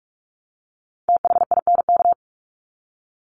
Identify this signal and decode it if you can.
Morse code, heard as a keyed tone, about 44 wpm, 700 Hz. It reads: THINK
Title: Top 300 Words 45wpm